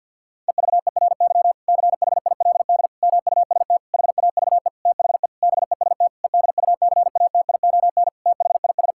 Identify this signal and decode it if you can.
Morse code, heard as a keyed tone, about 51 wpm, 700 Hz. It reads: EVERY CHILD MUST HAVE THE BEST EDUCATION THIS
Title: Sentences from Top 1000 Words - Review Part 6 - 50wpm